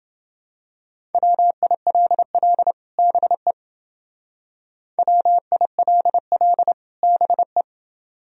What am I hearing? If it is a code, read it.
Morse code, heard as a keyed tone, about 29 wpm, 700 Hz. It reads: WILL BE WILL BE